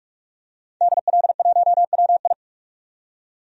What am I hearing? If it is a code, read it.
Morse code, heard as a keyed tone, about 41 wpm, 700 Hz. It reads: DL1WI
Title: DXCC-100 Call Signs - 5 Characters - 45wpm